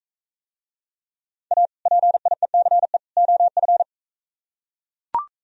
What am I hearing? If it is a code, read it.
Morse code, heard as a keyed tone, about 42 wpm, 700 Hz. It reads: A PIECE OF